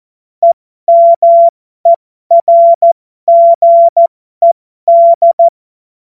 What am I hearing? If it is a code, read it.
Morse code, heard as a keyed tone, about 14 wpm, spaced as for 12 wpm, 680 Hz. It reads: EMERGED
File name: Morsekurs / Witze